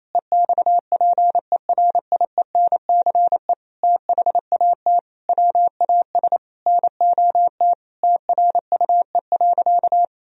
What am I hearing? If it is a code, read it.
Morse code, heard as a keyed tone, about 28 wpm, 700 Hz. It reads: EXPERIENCE THAT WAS NOT TRUE.
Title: Sentences from Top 1000 Words - Review Part 4 - 28wpm